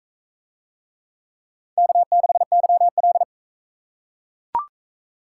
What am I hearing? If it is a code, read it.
Morse code, heard as a keyed tone, about 40 wpm, 700 Hz. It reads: K6YL